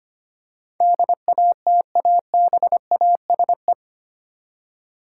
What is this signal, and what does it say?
Morse code, heard as a keyed tone, about 25 wpm, 700 Hz. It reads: DATABASE
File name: Eight Letter Words - 25wpm